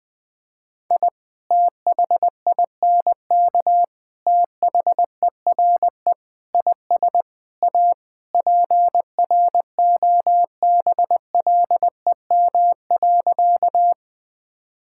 Morse code, heard as a keyed tone, about 20 wpm, 700 Hz. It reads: I THINK THERE IS A PROBLEM.
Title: Sentences with 5-Word Phrases - 20wpm